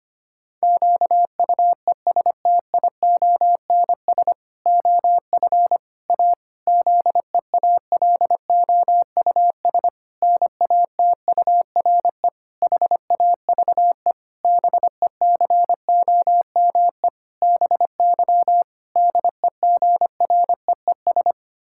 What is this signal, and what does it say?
Morse code, heard as a keyed tone, about 25 wpm, 700 Hz. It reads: QUESTIONS OF A ZEALOUS NATURE HAVE BECOME BY DEGREES
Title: Pangram Challenge v2 - 25wpm